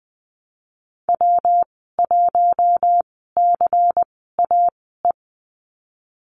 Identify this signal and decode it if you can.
Morse code, heard as a keyed tone, about 20 wpm, spaced as for 15 wpm, 700 Hz. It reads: W1CAE
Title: 179 Lesson 29g - Slash - Call Signs 20wpm